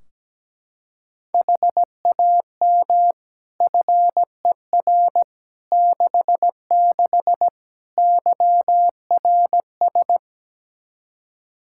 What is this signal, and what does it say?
Morse code, heard as a keyed tone, about 17 wpm, 700 Hz. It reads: HAM FER 66 YRS
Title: K1USN SST QSO Practice 17wpm